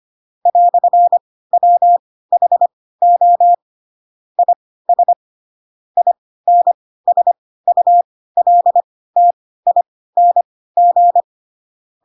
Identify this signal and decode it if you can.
Morse code, heard as a keyed tone, about 25 wpm, spaced as for 16 wpm, 700 Hz. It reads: "WHO IS INSULTING
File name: Pinocchio / Pinocchio ch2 10 25 ik1hge